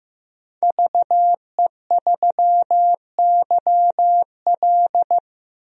Morse code, heard as a keyed tone, about 16 wpm, 700 Hz. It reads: VE3YL
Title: DXCC-100 Call Signs - 5 Characters - Repeated 3x - 15wpm